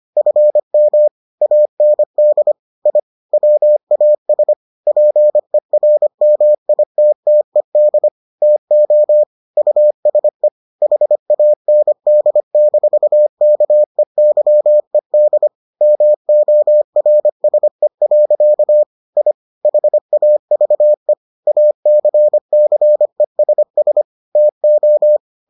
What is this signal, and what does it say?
Morse code, heard as a keyed tone, about 25 wpm, 600 Hz. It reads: FM AND I WAS PERMITTED TO USE HAND-KEYED MORSE. I HAVE ACCESS TO